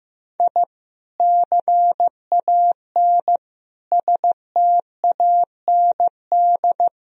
Morse code, heard as a keyed tone, about 15 wpm, 700 Hz. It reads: I CAN STAND
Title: Sentences from Top 1000 Words - Review Part 2 - 15wpm